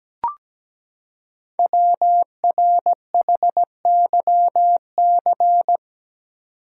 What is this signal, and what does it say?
Morse code, heard as a keyed tone, about 17 wpm, 700 Hz. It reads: WRHYC
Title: Code Groups - 5 Letters - 17wpm